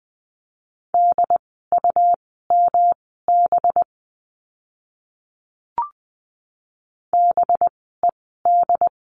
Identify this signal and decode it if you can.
Morse code, heard as a keyed tone, about 20 wpm, spaced as for 14 wpm, 700 Hz. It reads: DUMB BED
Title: 185 Lesson 30f - Letter b - Words 20wpm